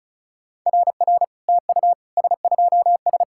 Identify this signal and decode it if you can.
Morse code, heard as a keyed tone, about 36 wpm, 700 Hz. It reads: RR TU S2S
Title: SOTA QSO Practice 35wpm